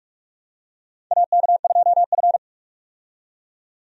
Morse code, heard as a keyed tone, about 46 wpm, 700 Hz. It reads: AK2F
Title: USA Call Signs - 4 Characters - 45wpm